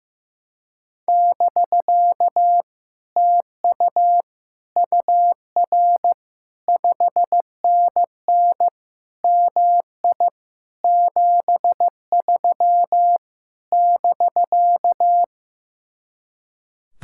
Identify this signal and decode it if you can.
Morse code, heard as a keyed tone, about 15 wpm, 700 Hz. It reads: <BK> TU UR 5NN MI 73 <BK>
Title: Speed-Racing Common QSO Phrases Encore - 15wpm